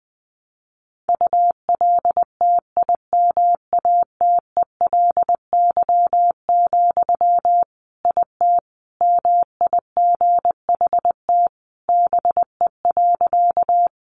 Morse code, heard as a keyed tone, about 20 wpm, 700 Hz. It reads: ULTIMATELY, IT MIGHT BE.
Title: Sentences from Top 1200 Words 20wpm